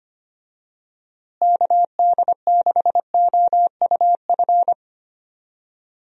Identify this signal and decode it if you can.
Morse code, heard as a keyed tone, about 25 wpm, 700 Hz. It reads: KD6OUF
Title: USA Call Signs - 6 Characters - Repeated 3x - 25wpm